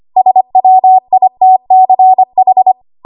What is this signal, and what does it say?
Morse code, heard as a keyed tone, about 25 wpm, 750 Hz. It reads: SWITCH
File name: Archive / 25 WPM / 25WPM